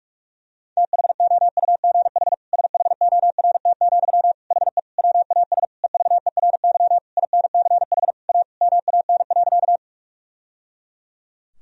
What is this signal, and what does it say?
Morse code, heard as a keyed tone, about 46 wpm, 700 Hz. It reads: THOUGH SHORT, HE WAS EVERY INCH A MAN.
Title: Sentences from Top 1000 Words - Review Part 3 - 45wpm